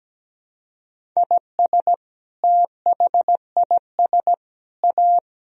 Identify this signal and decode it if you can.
Morse code, heard as a keyed tone, about 17 wpm, 700 Hz. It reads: IS THIS A
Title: Sentences from Top 1000 Words - Review Part 4 - 17wpm